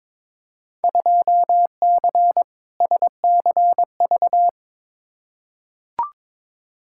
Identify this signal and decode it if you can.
Morse code, heard as a keyed tone, about 22 wpm, 700 Hz. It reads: 2C SCV
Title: Field Day Exchanges Encore - 22wpm